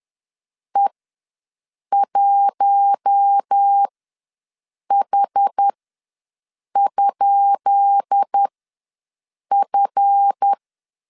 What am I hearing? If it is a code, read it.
Morse code, heard as a keyed tone, about 10 wpm, spaced as for 6 wpm, 800 Hz. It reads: E1H?F
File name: LEZIONI / 10 SIGNS / 10B INT ws 10wpm 4134